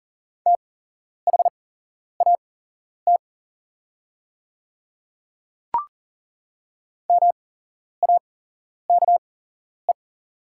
Morse code, heard as a keyed tone, about 40 wpm, spaced as for 11 wpm, 700 Hz. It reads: THAT MAKE
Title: 8x Character Spacing - Top 100 Words - 40wpm